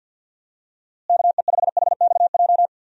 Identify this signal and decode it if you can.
Morse code, heard as a keyed tone, about 51 wpm, 700 Hz. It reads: KE5HXJ
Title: USA Call Signs - 6 Characters - Repeated 2x - 50wpm